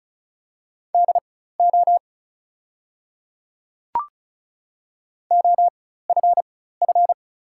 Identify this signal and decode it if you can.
Morse code, heard as a keyed tone, about 36 wpm, spaced as for 16 wpm, 700 Hz. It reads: DO OFF
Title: 4x Character Spacing - Top 200 Words - 35wpm